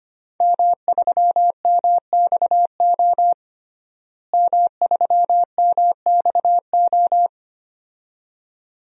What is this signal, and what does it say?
Morse code, heard as a keyed tone, about 25 wpm, 700 Hz. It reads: M3MXO M3MXO
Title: DXCC-100 Call Signs - 5 Characters - Repeated 2x - 25wpm